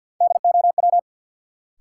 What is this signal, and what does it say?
Morse code, heard as a keyed tone, about 50 wpm, 700 Hz. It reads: DOW
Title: Sets of 5 Words - New Word Drill - 50wpm